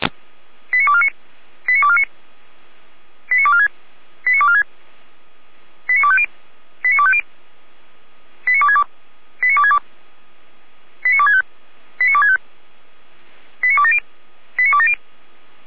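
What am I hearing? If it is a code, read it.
Five-tone selective call (ZVEI), heard as quick runs of five tones, tones of about 70 ms. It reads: 89249 89249 89246 89246 89250 89250 89262 89262 89265 89265 89270 89270
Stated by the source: ZVEI five-tone selective call received by SDR, hosted by feuerwehr-orlamuende.org (fire brigade Orlamünde).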